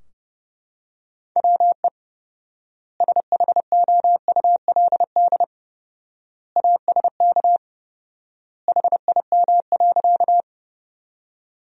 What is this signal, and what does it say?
Morse code, heard as a keyed tone, about 30 wpm, 700 Hz. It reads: WE SHOULD ASK HIM.
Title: 4x Word-Spacing - Sentences from Top 300 Words - 30wpm